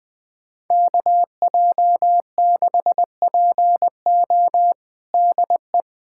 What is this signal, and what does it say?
Morse code, heard as a keyed tone, about 20 wpm, 700 Hz. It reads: KJ6PO DE
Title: USA Callsign DE Callsign - 5 Characters - 20wpm